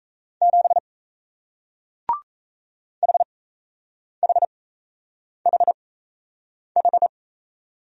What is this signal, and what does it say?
Morse code, heard as a keyed tone, about 38 wpm, spaced as for 8 wpm, 700 Hz. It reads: 7 HHHH